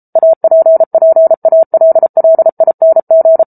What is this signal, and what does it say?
Morse code, heard as a keyed tone, about 34 wpm, 650 Hz. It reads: APPALLING